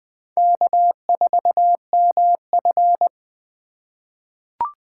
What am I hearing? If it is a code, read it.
Morse code, heard as a keyed tone, about 20 wpm, 700 Hz. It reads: K4MF
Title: 2023 SOTA Chaser Call Signs - 20wpm